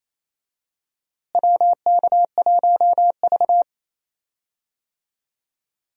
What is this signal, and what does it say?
Morse code, heard as a keyed tone, about 28 wpm, 700 Hz. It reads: WK1V
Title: USA Call Signs - 4 Characters - Repeated 2x - 28wpm